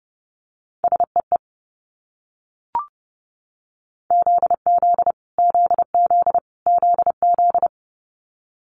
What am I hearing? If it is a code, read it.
Morse code, heard as a keyed tone, about 30 wpm, 700 Hz. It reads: SEE ZZ ZZ ZZ
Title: Multi-letter Phonemes Repeated 3x 30wpm